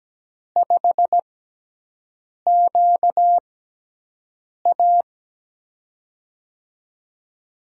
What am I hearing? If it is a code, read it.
Morse code, heard as a keyed tone, about 17 wpm, spaced as for 6 wpm, 700 Hz. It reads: 5QA
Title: USA Call Signs - 6x Character Spacing - 17wpm